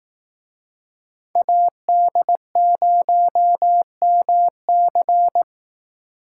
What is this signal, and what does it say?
Morse code, heard as a keyed tone, about 18 wpm, 700 Hz. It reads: AD0MC
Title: Speed Racing - USA 2x2 Call Signs 15wpm